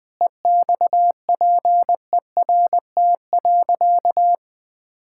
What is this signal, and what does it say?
Morse code, heard as a keyed tone, about 21 wpm, 700 Hz. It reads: EXPERT.
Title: Sentences with 5-Word Phrases - 20wpm